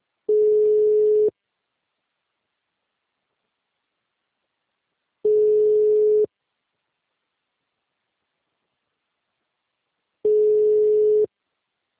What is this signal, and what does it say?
Ringback tone (Europe, 425 Hz).
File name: out- -2007- - - .118